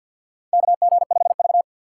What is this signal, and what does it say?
Morse code, heard as a keyed tone, about 51 wpm, 700 Hz. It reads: KG5V
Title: K1USN SST QSO Practice - 50wpm